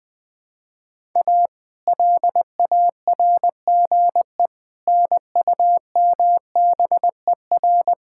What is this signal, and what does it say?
Morse code, heard as a keyed tone, about 21 wpm, 700 Hz. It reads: A LARGE NUMBER
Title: Build A Phrase - 4-Word Phrases - 20wpm